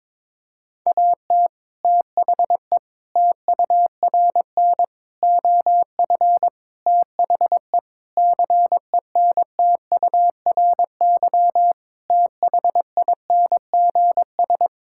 Morse code, heard as a keyed tone, about 22 wpm, 700 Hz. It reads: AT THE TURN OF THE CENTURY THINGS
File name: practice / 22wpm / Sentences with 4-Word Phrases - 22wpm